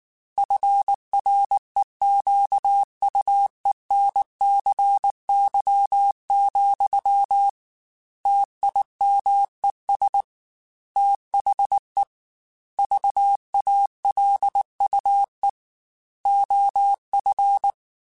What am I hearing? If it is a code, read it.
Morse code, heard as a keyed tone, about 19 wpm, 800 Hz. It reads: FREQUENCY, TIMES THE VALUE OF
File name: Technical Subjects / Technical Subjects-mp3s / XC @18